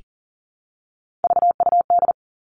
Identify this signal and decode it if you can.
Morse code, heard as a keyed tone, about 42 wpm, 700 Hz. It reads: VUD
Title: Three Challenging Letter-Number Combinations 40wpm